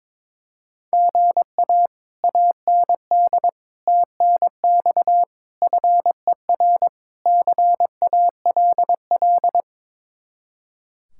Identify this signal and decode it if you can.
Morse code, heard as a keyed tone, about 22 wpm, 700 Hz. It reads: GA AND TNX FER CALL
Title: Speed-Racing Common QSO Phrases Encore - 22wpm